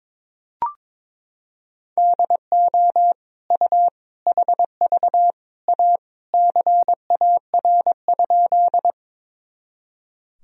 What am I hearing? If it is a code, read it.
Morse code, heard as a keyed tone, about 22 wpm, 700 Hz. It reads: DO U HV A CAR?
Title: Sentences from Top 300 Words and Abbreviations 22wpm